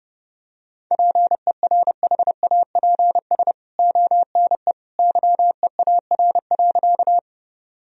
Morse code, heard as a keyed tone, about 32 wpm, 700 Hz. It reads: PERHAPS ONE YEAR.
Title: Sentences from Top 1000 Words - Review Part 1 - 30wpm